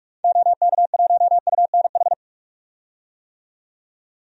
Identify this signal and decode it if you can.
Morse code, heard as a keyed tone, about 46 wpm, 700 Hz. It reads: OK1UNH
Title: DXCC-100 Call Signs - 6 Characters - 45wpm